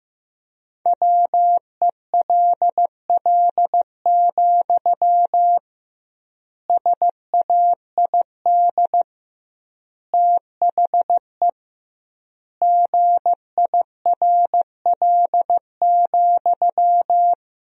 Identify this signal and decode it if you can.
Morse code, heard as a keyed tone, about 15 wpm, 700 Hz. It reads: WELL, SAID THE GIRL,
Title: 2x Word-Spacing - Sentences from Top 400 Words - 15wpm